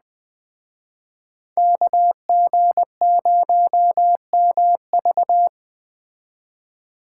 Morse code, heard as a keyed tone, about 21 wpm, 700 Hz. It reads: KG0MV
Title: Speed Racing - USA 2x2 Call Signs 17wpm